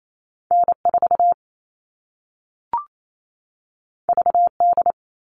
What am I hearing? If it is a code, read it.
Morse code, heard as a keyed tone, about 28 wpm, 700 Hz. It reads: N4 VD
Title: All Cut Numbers - 2-Digit 28wpm